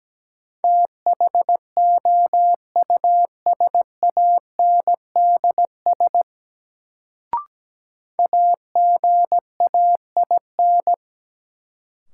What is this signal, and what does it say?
Morse code, heard as a keyed tone, about 17 wpm, 700 Hz. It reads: THOUSANDS AGAIN
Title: Sets of 4 Words - New Word Drill - 17wpm